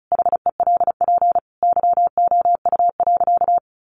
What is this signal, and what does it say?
Morse code, heard as a keyed tone, about 34 wpm, 700 Hz. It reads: HELP YOU.
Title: Sentences from Top 1200 Words 35wpm